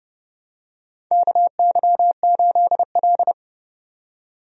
Morse code, heard as a keyed tone, about 29 wpm, 700 Hz. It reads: KY8L
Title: USA Call Signs - 4 Characters - 30wpm